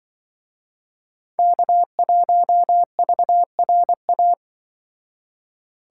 Morse code, heard as a keyed tone, about 24 wpm, 700 Hz. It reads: K1VRA
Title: Speed Racing - USA 1x3 Call Signs 20wpm